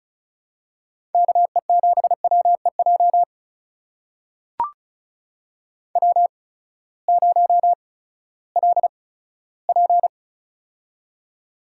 Morse code, heard as a keyed tone, about 36 wpm, spaced as for 18 wpm, 700 Hz. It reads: KE7WEJ W0LP
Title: USA Call Signs - 8x Character Spacing - 35wpm